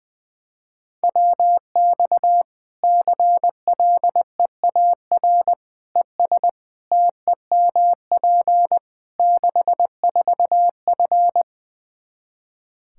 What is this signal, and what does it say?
Morse code, heard as a keyed tone, about 21 wpm, 700 Hz. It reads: WX CLEAR ES TEMP 64F
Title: K1USN SST QSO Practice 20wpm